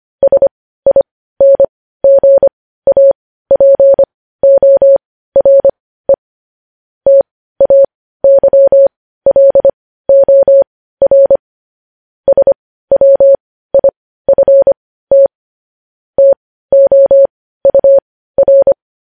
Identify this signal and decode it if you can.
Morse code, heard as a keyed tone, about 25 wpm, spaced as for 15 wpm, 550 Hz. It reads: SINGAPORE TAYLOR SWIFT TOUR
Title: News Headlines at 15